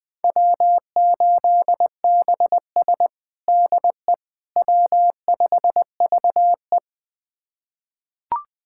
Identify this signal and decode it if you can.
Morse code, heard as a keyed tone, about 20 wpm, 700 Hz. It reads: W8BS DE W5VE
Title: USA Callsign DE Callsign - 4 Characters - 20wpm